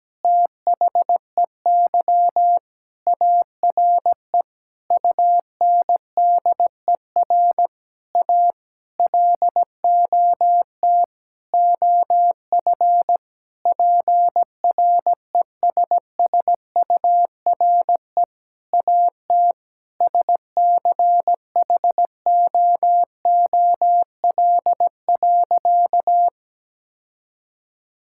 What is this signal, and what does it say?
Morse code, heard as a keyed tone, about 17 wpm, 700 Hz. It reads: THEY ARE UNDER A LOT OF PRESSURE AT SCHOOL.
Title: Sentences from Top 1000 Words - Review Part 6 - 17wpm